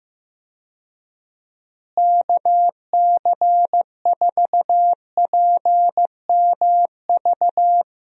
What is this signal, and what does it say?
Morse code, heard as a keyed tone, about 15 wpm, 700 Hz. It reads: KC4PMV
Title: USA Call Signs - 6 Characters - Repeated 3x - 15wpm